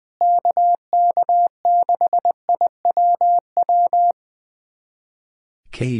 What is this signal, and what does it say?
Morse code, heard as a keyed tone, about 20 wpm, 700 Hz. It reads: KK6IWW
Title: USA Callsign DE Callsign - 6 Characters - 20wpm